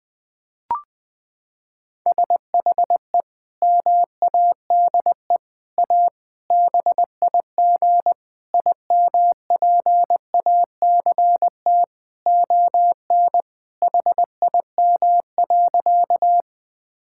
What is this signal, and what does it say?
Morse code, heard as a keyed tone, about 21 wpm, 700 Hz. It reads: SHE MADE A BIG IMPACT ON HIM.
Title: Sentences from Top 1000 Words - Review Part 6 - 20wpm